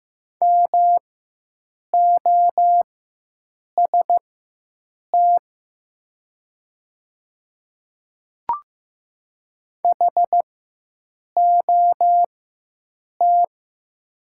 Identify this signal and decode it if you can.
Morse code, heard as a keyed tone, about 15 wpm, spaced as for 7 wpm, 700 Hz. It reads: MOST HOT
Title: 4x Character Spacing - Top 100 Words - 15wpm